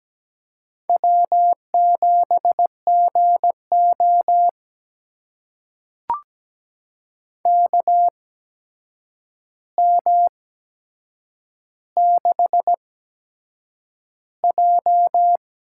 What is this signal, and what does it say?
Morse code, heard as a keyed tone, about 17 wpm, spaced as for 7 wpm, 700 Hz. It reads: W7GO KM6J
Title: USA Call Signs - 8x Character Spacing - 17wpm